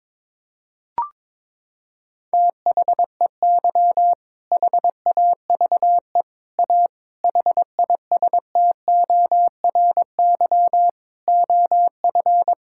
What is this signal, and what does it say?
Morse code, heard as a keyed tone, about 22 wpm, 700 Hz. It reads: THEY HAVE A HISTORY OF
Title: Sentences from Top 1000 Words - Review Part 5 - 22wpm